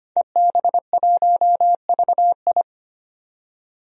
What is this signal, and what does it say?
Morse code, heard as a keyed tone, about 25 wpm, 700 Hz. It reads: EB1VI
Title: DXCC-100 Call Signs - 5 Characters - 25wpm